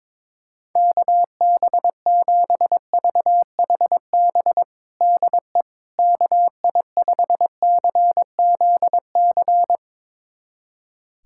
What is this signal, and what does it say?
Morse code, heard as a keyed tone, about 22 wpm, 700 Hz. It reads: KB7VHB DE KI5CZC
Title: USA Callsign DE Callsign - 6 Characters - 22wpm